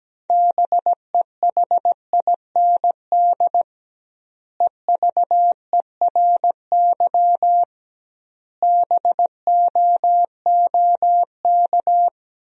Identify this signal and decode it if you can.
Morse code, heard as a keyed tone, about 17 wpm, 700 Hz. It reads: BEHIND EVERY BOOK
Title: 2x Word-Spacing - Sentences from Top 500 Words - 17wpm